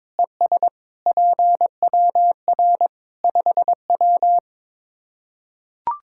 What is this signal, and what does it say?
Morse code, heard as a keyed tone, about 22 wpm, 700 Hz. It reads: ES PWR 5W
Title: Practice QSO - Exchange 2 - Info - 22wpm